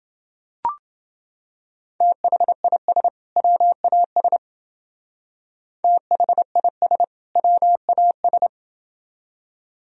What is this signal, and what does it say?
Morse code, heard as a keyed tone, about 28 wpm, 700 Hz. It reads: THIS WAS THIS WAS